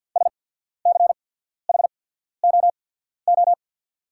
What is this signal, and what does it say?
Morse code, heard as a keyed tone, about 48 wpm, spaced as for 14 wpm, 690 Hz. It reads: SCHOO
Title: 8x Character Spacing - Top 300 Words - 50wpm